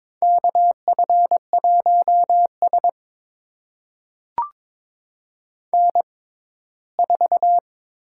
Morse code, heard as a keyed tone, about 22 wpm, spaced as for 15 wpm, 700 Hz. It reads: KF1S N4
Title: USA Call Signs - 4 Characters - 6x Character Spacing - 22wpm